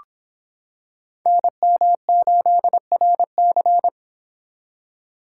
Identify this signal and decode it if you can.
Morse code, heard as a keyed tone, about 26 wpm, 700 Hz. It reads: NM8RC